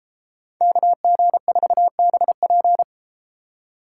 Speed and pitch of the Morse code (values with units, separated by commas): 33 wpm, 700 Hz